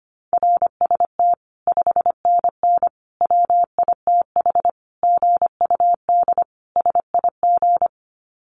Morse code, heard as a keyed tone, about 25 wpm, 700 Hz. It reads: RST 5NN WITH GUD SIG
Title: Common QSO Phrases 25wpm